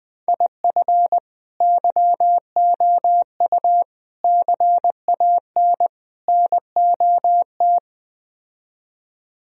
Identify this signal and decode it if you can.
Morse code, heard as a keyed tone, about 20 wpm, 700 Hz. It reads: IF YOU CAN NOT